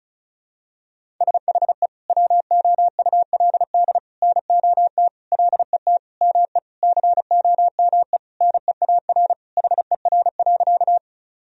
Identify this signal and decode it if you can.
Morse code, heard as a keyed tone, about 34 wpm, 700 Hz. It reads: SHE WOULD NOT LET ME COME NEAR HER.
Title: Sentences from Top 1000 Words - Review Part 1 - 35wpm